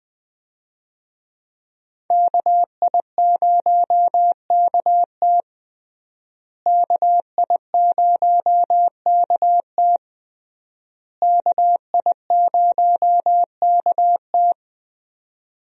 Morse code, heard as a keyed tone, about 21 wpm, 700 Hz. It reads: KI0KT KI0KT KI0KT
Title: USA Call Signs - 5 Characters - Repeated 3x - 20wpm